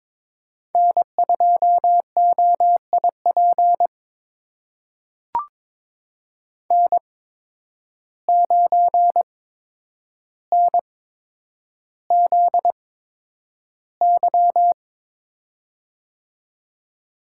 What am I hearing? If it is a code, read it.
Morse code, heard as a keyed tone, about 22 wpm, spaced as for 9 wpm, 700 Hz. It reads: N2OIP N9NZY
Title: USA Call Signs - 5 Characters - 8x Character Spacing - 22wpm